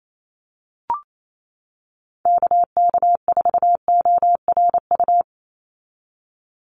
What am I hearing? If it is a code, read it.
Morse code, heard as a keyed tone, about 28 wpm, 700 Hz. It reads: KK4ORU
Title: USA Call Signs 28wpm